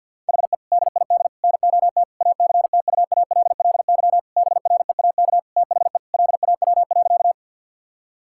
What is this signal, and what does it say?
Morse code, heard as a keyed tone, about 48 wpm, 700 Hz. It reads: HE DID NOT ACTUALLY BREAK THE LAW.
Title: Sentences from Top 1000 Words - Review Part 6 - 50wpm